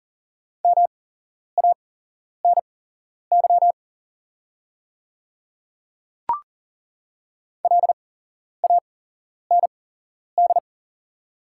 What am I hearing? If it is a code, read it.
Morse code, heard as a keyed tone, about 39 wpm, spaced as for 11 wpm, 700 Hz. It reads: MANY LAND
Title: 8x Character Spacing - Top 200 Words - 40wpm